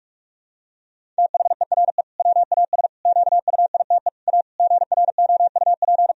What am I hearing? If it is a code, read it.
Morse code, heard as a keyed tone, about 42 wpm, 700 Hz. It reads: THERE WAS QUITE A GROUP
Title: Sentences from Top 1000 Words - Review Part 1 - 45wpm